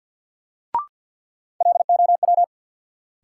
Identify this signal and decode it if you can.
Morse code, heard as a keyed tone, about 50 wpm, 700 Hz. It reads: LOW
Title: Sentences from Top 500 Words - New Word Drill - 50wpm